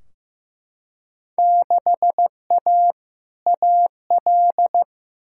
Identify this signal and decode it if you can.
Morse code, heard as a keyed tone, about 15 wpm, 700 Hz. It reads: 6A AL
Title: Field Day Exchanges Encore - 15wpm